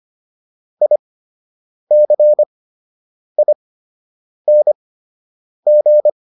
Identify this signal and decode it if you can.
Morse code, heard as a keyed tone, about 25 wpm, spaced as for 8 wpm, 600 Hz. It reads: ICING